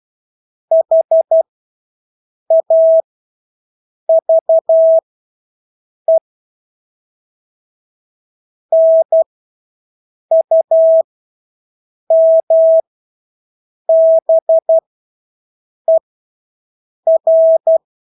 Morse code, heard as a keyed tone, about 12 wpm, spaced as for 6 wpm, 650 Hz. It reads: HAVE NUMBER
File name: Engelske ord 12-6 wpm